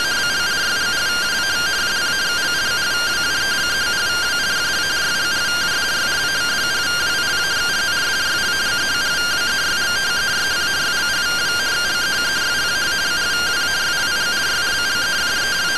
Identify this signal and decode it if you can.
Two-tone teleprinter signal, about 170 Hz shift.
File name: findings / MR CORONAL 55.gen-15 rtty rx